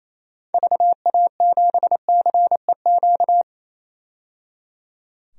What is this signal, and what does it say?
Morse code, heard as a keyed tone, about 28 wpm, 700 Hz. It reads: VA7CEQ
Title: DXCC-100 Call Signs - 6 Characters - 28wpm